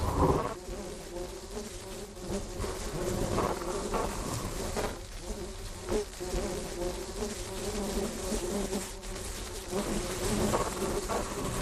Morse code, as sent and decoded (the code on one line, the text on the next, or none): none